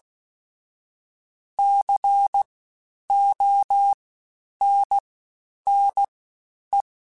Morse code, heard as a keyed tone, about 16 wpm, 800 Hz. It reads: C O N N E
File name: Special INGs-mp3s / Special INGs-mp3s / Fast ing @15